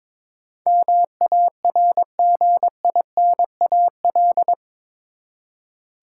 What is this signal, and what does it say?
Morse code, heard as a keyed tone, about 22 wpm, 700 Hz. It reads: MARGINAL
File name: practice / 22wpm / Eight Letter Words - 22wpm